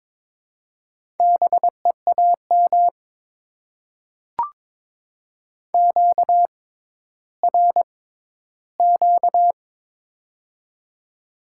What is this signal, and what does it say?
Morse code, heard as a keyed tone, about 22 wpm, spaced as for 13 wpm, 700 Hz. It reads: BEAM QRQ
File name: Top 90 QSO Elements - 6x Character Spacing - 22wpm